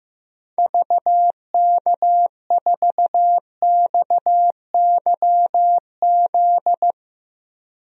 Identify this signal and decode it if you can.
Morse code, heard as a keyed tone, about 15 wpm, 700 Hz. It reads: VK4XYZ
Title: DXCC-100 Call Signs - 6 Characters - Repeated 3x - 15wpm